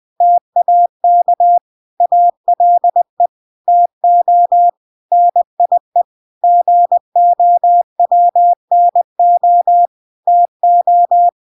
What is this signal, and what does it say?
Morse code, heard as a keyed tone, about 20 wpm, 700 Hz. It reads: TAK ALE TO NIE GOWNO TO